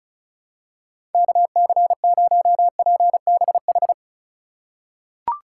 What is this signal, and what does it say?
Morse code, heard as a keyed tone, about 36 wpm, 700 Hz. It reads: KC0PBH
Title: USA Call Signs - 4x Character Spacing - 35wpm